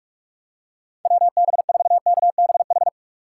Morse code, heard as a keyed tone, about 46 wpm, 690 Hz. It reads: WB4KBH
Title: USA Call Signs - 6 Characters - 2x Character Spacing - 45wpm